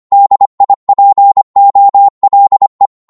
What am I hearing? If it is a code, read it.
Morse code, heard as a keyed tone, about 25 wpm, 800 Hz. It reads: DIPOLE
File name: Morse / qso-25wpm